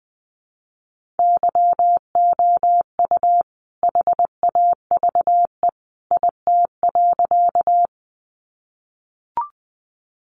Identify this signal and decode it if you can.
Morse code, heard as a keyed tone, about 20 wpm, 700 Hz. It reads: YOU HAVE IT.
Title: Sentences from Top 100 Words 20wpm